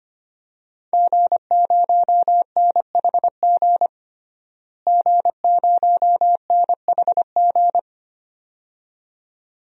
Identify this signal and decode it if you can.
Morse code, heard as a keyed tone, about 25 wpm, 700 Hz. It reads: G0NHG G0NHG
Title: DXCC-100 Call Signs - 5 Characters - Repeated 2x - 25wpm